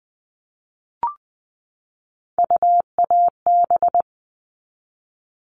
Morse code, heard as a keyed tone, about 20 wpm, 700 Hz. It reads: UAB
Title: All Cut Numbers - 3-Digit 20wpm